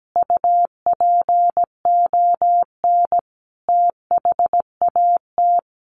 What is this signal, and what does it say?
Morse code, heard as a keyed tone, about 17 wpm, 700 Hz. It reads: UPON THAT
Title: Sentences from Top 1500 Words 17wpm